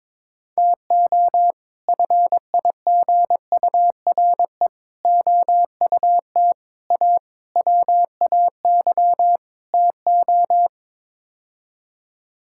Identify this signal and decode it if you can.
Morse code, heard as a keyed tone, about 22 wpm, 700 Hz. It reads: TO FIGURE OUT A WAY TO
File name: practice / 22wpm / Sets of 6 Words - 22wpm